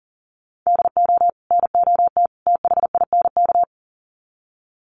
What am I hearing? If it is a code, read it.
Morse code, heard as a keyed tone, about 40 wpm, 700 Hz. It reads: DO NOT THINK
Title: Sets of 3 Words 40wpm